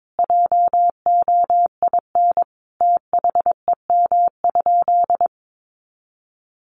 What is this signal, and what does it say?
Morse code, heard as a keyed tone, about 22 wpm, 700 Hz. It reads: JOIN THEM?
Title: Sentences from Top 1400 Words 22wpm